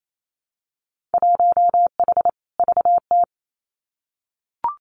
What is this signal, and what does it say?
Morse code, heard as a keyed tone, about 28 wpm, 700 Hz. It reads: 1H VT
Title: Winter Field Day Exchanges Encore 28wpm